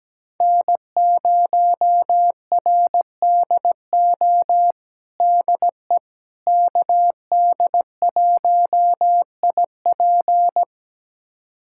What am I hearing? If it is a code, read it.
Morse code, heard as a keyed tone, about 17 wpm, 700 Hz. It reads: N0RDO DE KD1IP
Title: USA Callsign DE Callsign - 5 Characters - 17wpm